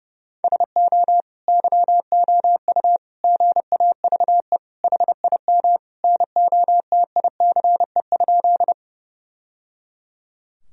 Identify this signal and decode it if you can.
Morse code, heard as a keyed tone, about 30 wpm, 700 Hz. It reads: SO YOU GAVE HIM NOTICE?
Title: Sentences from Top 1000 Words - Review Part 2 - 30wpm